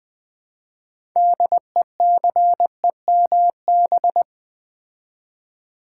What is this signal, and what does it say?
Morse code, heard as a keyed tone, about 20 wpm, 700 Hz. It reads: DECEMB
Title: Build A Word - 2 Letters At A Time - 8-Letter Words - 20wpm